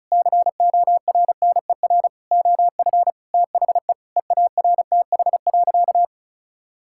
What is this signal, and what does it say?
Morse code, heard as a keyed tone, about 34 wpm, 700 Hz. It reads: CORNER OF THE EARTH.
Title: Sentences from Top 1600 Words 35wpm